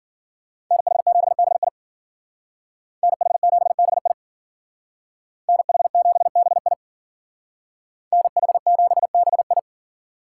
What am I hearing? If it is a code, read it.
Morse code, heard as a keyed tone, about 52 wpm, 700 Hz. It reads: NH7BI NH7BI NH7BI NH7BI